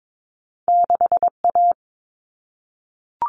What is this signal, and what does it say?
Morse code, heard as a keyed tone, about 22 wpm, 700 Hz. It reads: 6A
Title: All Cut Numbers - 2-Digit 22wpm